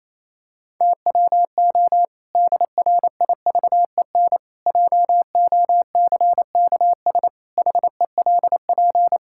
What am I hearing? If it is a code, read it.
Morse code, heard as a keyed tone, about 28 wpm, 700 Hz. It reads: TWO DRIVEN JOCKS HELP
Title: Pangram Challenge v2 - 28wpm